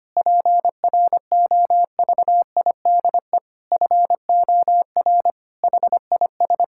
Morse code, heard as a keyed tone, about 25 wpm, 700 Hz. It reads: PROVIDE FOR HIS
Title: Sentences from Top 1000 Words - Review Part 5 - 25wpm